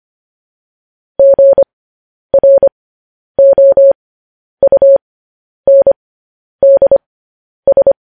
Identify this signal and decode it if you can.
Morse code, heard as a keyed tone, about 25 wpm, spaced as for 10 wpm, 550 Hz. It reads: GROUNDS